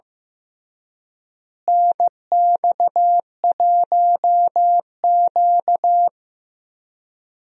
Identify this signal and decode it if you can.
Morse code, heard as a keyed tone, about 15 wpm, 700 Hz. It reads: NX1Q
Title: USA Call Signs - 4 Characters - Repeated 3x - 15wpm